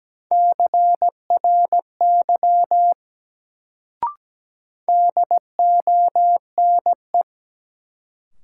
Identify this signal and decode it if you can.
Morse code, heard as a keyed tone, about 17 wpm, 700 Hz. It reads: CRY DONE
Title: Sentences from Top 500 Words - New Word Drill - 17wpm